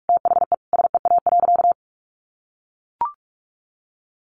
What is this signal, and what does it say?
Morse code, heard as a keyed tone, about 44 wpm, 700 Hz. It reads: THE SEA.
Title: Sentences from Top 1400 Words 45wpm